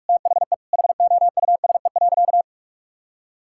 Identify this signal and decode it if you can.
Morse code, heard as a keyed tone, about 46 wpm, 700 Hz. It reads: THE HOUSE.
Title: Sentences from Top 1000 Words - Review Part 6 - 45wpm